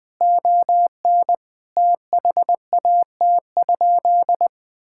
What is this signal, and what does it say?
Morse code, heard as a keyed tone, about 21 wpm, 700 Hz. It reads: ON THAT?
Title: Sentences from Top 1600 Words 20wpm